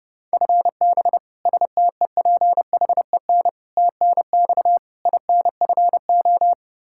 Morse code, heard as a keyed tone, about 30 wpm, 700 Hz. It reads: FB STEPHEN TNX INFO
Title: Practice QSO - Exchange 3 - More Info - 30wpm